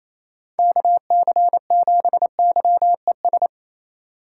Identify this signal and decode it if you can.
Morse code, heard as a keyed tone, about 28 wpm, 700 Hz. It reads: KC7YES
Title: USA Call Signs - 6 Characters - 4x Character Spacing - 28wpm